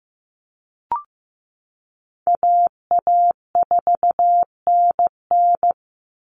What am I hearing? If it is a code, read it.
Morse code, heard as a keyed tone, about 15 wpm, 700 Hz. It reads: AA4NN
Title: Call Signs Repeated 3x - 15wpm